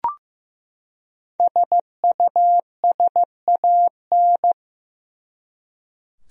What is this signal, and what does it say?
Morse code, heard as a keyed tone, about 15 wpm, 700 Hz. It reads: SUSAN
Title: USA Common Names - Max 5 Letters - 15wpm